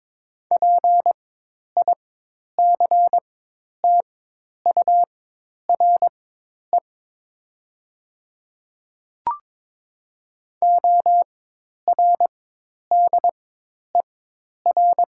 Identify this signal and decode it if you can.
Morse code, heard as a keyed tone, about 21 wpm, spaced as for 10 wpm, 700 Hz. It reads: PICTURE ORDER
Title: 4x Character Spacing - Top 500 Words - 22wpm